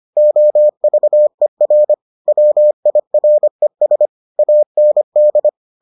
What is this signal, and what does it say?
Morse code, heard as a keyed tone, about 25 wpm, 600 Hz. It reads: OVER WIRES AND